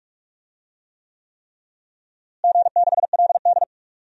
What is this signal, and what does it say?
Morse code, heard as a keyed tone, about 45 wpm, 700 Hz. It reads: G6LD
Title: DXCC-100 Call Signs - 4 Characters - Repeated 3x - 45wpm